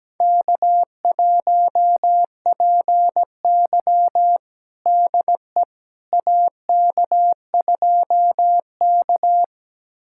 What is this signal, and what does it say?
Morse code, heard as a keyed tone, about 17 wpm, 700 Hz. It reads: K1PY DE AK2K
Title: USA Callsign DE Callsign - 4 Characters - 17wpm